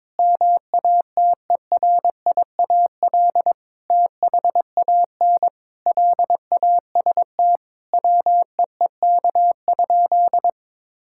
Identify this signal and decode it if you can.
Morse code, heard as a keyed tone, about 22 wpm, 700 Hz. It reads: MATERIAL THAN LAST WEEK?
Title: Sentences from Top 1000 Words - Review Part 5 - 22wpm